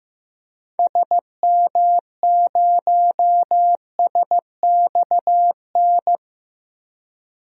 Morse code, heard as a keyed tone, about 15 wpm, 700 Hz. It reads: SM0SXN
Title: DXCC-100 Call Signs - 6 Characters - 15wpm